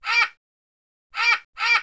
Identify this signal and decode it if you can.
There is no code.